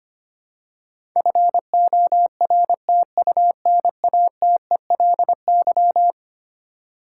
Morse code, heard as a keyed tone, about 25 wpm, 700 Hz. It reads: FORTUNATELY
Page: morse-code-ninja-practice-25wpm